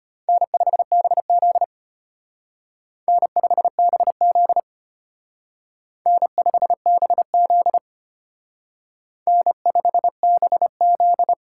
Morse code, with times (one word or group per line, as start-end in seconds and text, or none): 0.3-1.6 N5BZ
3.1-4.6 N5BZ
6.1-7.8 N5BZ
9.3-11.3 N5BZ